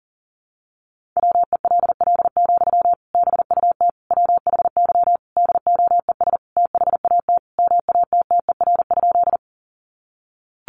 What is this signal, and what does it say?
Morse code, heard as a keyed tone, about 40 wpm, 700 Hz. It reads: WELL, BUT WHY DOES THAT MATTER?
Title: Sentences from Top 700 Words 40wpm